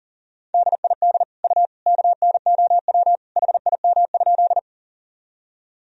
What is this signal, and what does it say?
Morse code, heard as a keyed tone, about 40 wpm, 700 Hz. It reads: DID U KNOW HIM?
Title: Sentences from Top 100 Words and Abbreviations 40wpm